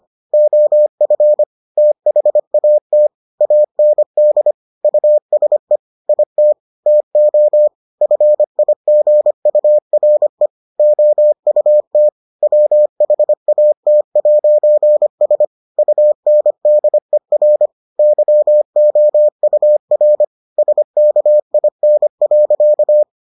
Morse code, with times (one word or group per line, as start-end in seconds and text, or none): 0.3-1.4 OF
1.8-3.1 THAT
3.4-4.5 AND
4.8-5.8 USE
6.1-6.5 IT
6.9-7.7 TO
8.0-10.5 FIGURE
10.8-12.1 OUT
12.4-15.4 WHAT'S
15.8-17.7 UNDER
18.0-20.2 YOUR
20.6-23.0 SKIN.